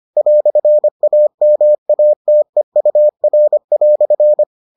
Morse code, heard as a keyed tone, about 25 wpm, 600 Hz. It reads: "AMATEUR"